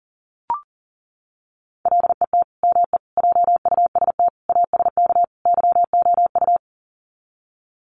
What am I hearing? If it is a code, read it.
Morse code, heard as a keyed tone, about 40 wpm, 700 Hz. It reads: LET ME JUST ASK YOU
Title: Sets of 5 Words Encore2 40wpm